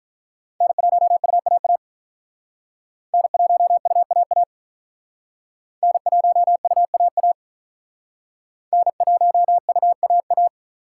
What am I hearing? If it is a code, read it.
Morse code, heard as a keyed tone, about 47 wpm, 700 Hz. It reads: N1UAA N1UAA N1UAA N1UAA